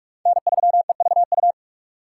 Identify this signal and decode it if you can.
Morse code, heard as a keyed tone, about 44 wpm, 700 Hz. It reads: N3EVU
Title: USA Call Signs - 5 Characters - 3x Character Spacing - 45wpm